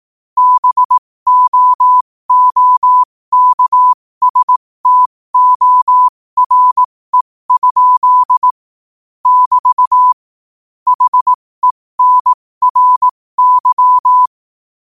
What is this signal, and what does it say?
Morse code, heard as a keyed tone, about 18 wpm, spaced as for 15 wpm, 1000 Hz. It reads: BOOKSTORE? <BT> HENRY
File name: qotdfiles / 15wpm